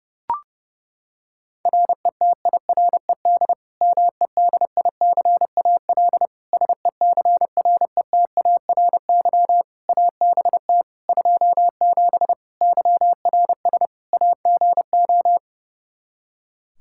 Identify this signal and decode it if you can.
Morse code, heard as a keyed tone, about 32 wpm, 700 Hz. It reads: RETIRED MEDICAL SECRETARY ABT 27 YRS AGO
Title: Practice QSO - Exchange 3 - More Info - 30wpm